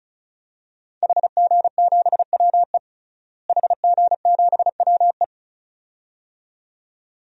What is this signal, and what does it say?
Morse code, heard as a keyed tone, about 34 wpm, 700 Hz. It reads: HG7WE HG7WE
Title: DXCC-100 Call Signs - 5 Characters - Repeated 2x - 35wpm